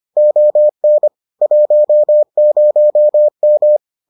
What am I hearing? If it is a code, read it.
Morse code, heard as a keyed tone, about 25 wpm, 600 Hz. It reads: ON 10M